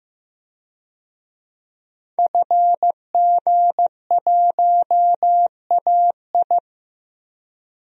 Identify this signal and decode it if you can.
Morse code, heard as a keyed tone, about 15 wpm, 700 Hz. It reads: FG1AI
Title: DXCC-100 Call Signs - 5 Characters - 15wpm